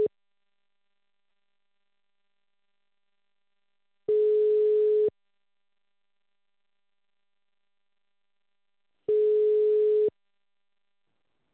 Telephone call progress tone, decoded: ringback tone